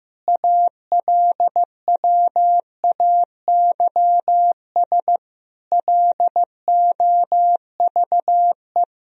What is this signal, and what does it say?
Morse code, heard as a keyed tone, about 15 wpm, 700 Hz. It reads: ALWAYS LOVE
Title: Sentences from Top 400 Words and Abbreviations 15wpm